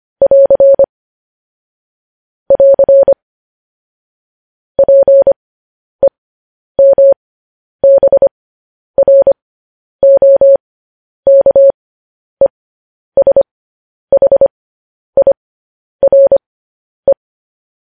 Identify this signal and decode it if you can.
Morse code, heard as a keyed tone, about 25 wpm, spaced as for 10 wpm, 550 Hz. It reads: <AR> <AR> PEMBROKESHIRE
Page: news headlines in morse code at 10 wpm